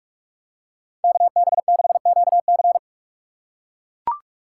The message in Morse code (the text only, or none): KB6QC